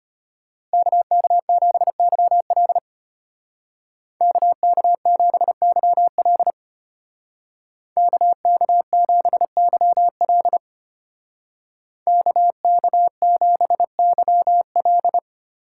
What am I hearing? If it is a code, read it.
Morse code, heard as a keyed tone, about 33 wpm, 700 Hz. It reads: KK7YL KK7YL KK7YL KK7YL